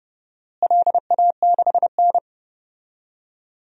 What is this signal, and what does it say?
Morse code, heard as a keyed tone, about 31 wpm, 700 Hz. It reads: LA6N
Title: DXCC-100 Call Signs - 4 Characters - Repeated 3x - 30wpm